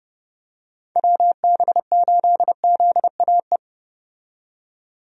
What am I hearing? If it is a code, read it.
Morse code, heard as a keyed tone, about 30 wpm, 700 Hz. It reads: WB8ZAE
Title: USA Call Signs - 6 Characters - 30wpm